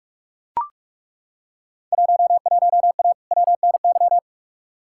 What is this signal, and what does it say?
Morse code, heard as a keyed tone, about 43 wpm, 700 Hz. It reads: 11A WNY
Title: Field Day Exchanges Encore - 45wpm